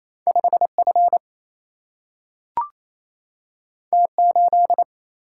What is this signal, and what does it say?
Morse code, heard as a keyed tone, about 28 wpm, 700 Hz. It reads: 5F T8
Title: Single-Letter Single-Number 28wpm